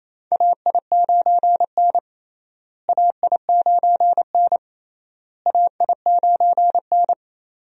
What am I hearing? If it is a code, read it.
Morse code, heard as a keyed tone, about 28 wpm, 700 Hz. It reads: AI9N AI9N AI9N
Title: USA Call Signs - 4 Characters - Repeated 3x - 28wpm